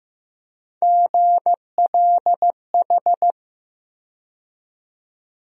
Morse code, heard as a keyed tone, about 15 wpm, 700 Hz. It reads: GLH
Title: USA Call Sign Postfixes - 3 Letters 15wpm